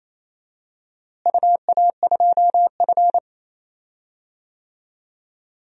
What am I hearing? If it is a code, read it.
Morse code, heard as a keyed tone, about 27 wpm, 700 Hz. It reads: UA2F
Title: DXCC-100 Call Signs - 4 Characters - Repeated 3x - 28wpm